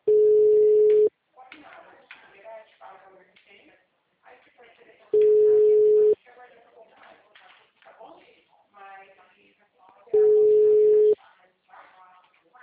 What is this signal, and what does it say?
Ringback tone (Europe, 425 Hz), with speech.